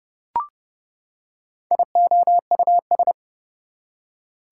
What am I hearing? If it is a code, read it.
Morse code, heard as a keyed tone, about 30 wpm, 700 Hz. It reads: IOUS
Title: Prefixes and Suffixes 30wpm